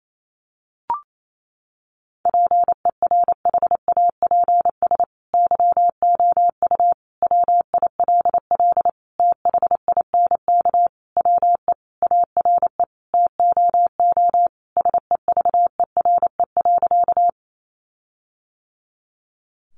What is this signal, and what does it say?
Morse code, heard as a keyed tone, about 28 wpm, 700 Hz. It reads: PERHAPS YOU WILL THINK WE ARE TOO SEVERE.
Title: Sentences from Top 1400 Words 28wpm